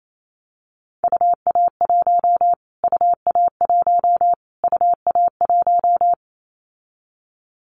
Morse code, heard as a keyed tone, about 28 wpm, 700 Hz. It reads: UA1 UA1 UA1
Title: Callsign Prefixes for Top 25 DXCC - Repeated 3x - 28wpm